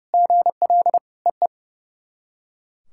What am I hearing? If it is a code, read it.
Morse code, heard as a keyed tone, about 30 wpm, 690 Hz. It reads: GL EE
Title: SOTA QSO Practice 30wpm